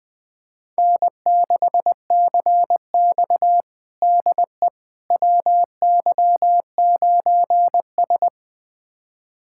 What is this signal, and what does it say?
Morse code, heard as a keyed tone, about 21 wpm, 700 Hz. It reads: N6CX DE WY9S
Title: USA Callsign DE Callsign - 4 Characters - 20wpm